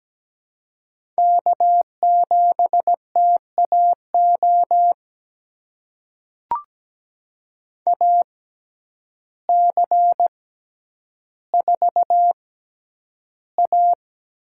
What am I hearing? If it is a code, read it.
Morse code, heard as a keyed tone, about 17 wpm, spaced as for 9 wpm, 700 Hz. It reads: K7TAO AC4A
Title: USA Call Signs - 5 Characters - 6x Character Spacing - 17wpm